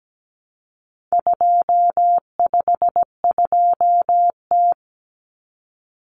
Morse code, heard as a keyed tone, about 17 wpm, 700 Hz. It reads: 252T
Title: Common Cut Numbers - 4-Digit 17wpm